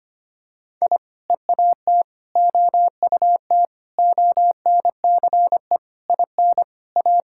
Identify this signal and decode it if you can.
Morse code, heard as a keyed tone, about 25 wpm, 700 Hz. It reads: I EAT OUT ONCE IN A
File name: practice / 25wpm / Sentences with 4-Word Phrases - 25wpm